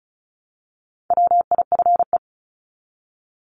Morse code, heard as a keyed tone, about 34 wpm, 700 Hz. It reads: WIFE
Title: Top 1751-2000 Words 35wpm